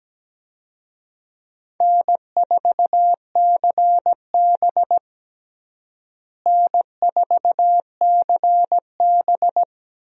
Morse code, heard as a keyed tone, about 17 wpm, 700 Hz. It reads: N4CB N4CB
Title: USA Call Signs - 4 Characters - Repeated 3x - 17wpm